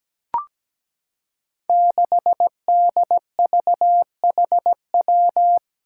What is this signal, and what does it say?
Morse code, heard as a keyed tone, about 17 wpm, 700 Hz. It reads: 6DVHW
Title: Five Challenging Letter-Number Combinations 17wpm